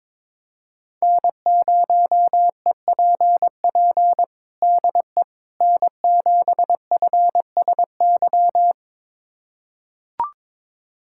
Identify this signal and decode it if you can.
Morse code, heard as a keyed tone, about 21 wpm, 700 Hz. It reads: N0EPP DE N7FSY
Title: USA Callsign DE Callsign - 5 Characters - 22wpm